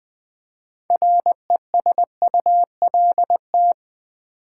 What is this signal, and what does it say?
Morse code, heard as a keyed tone, about 20 wpm, 700 Hz. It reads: RESULT
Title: Build A Word - 2 Letters At A Time - 6-Letter Words - 20wpm